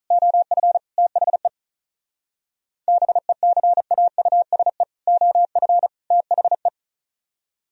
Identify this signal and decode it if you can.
Morse code, heard as a keyed tone, about 43 wpm, 700 Hz. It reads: OF THE BECAUSE OF THE
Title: Speed-Racing - Sets of 3 Words 35wpm